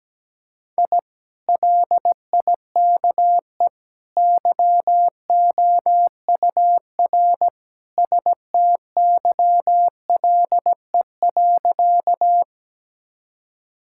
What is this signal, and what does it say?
Morse code, heard as a keyed tone, about 17 wpm, 700 Hz. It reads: I LIKE YOUR STYLE.
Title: Sentences from Top 1000 Words - Review Part 5 - 17wpm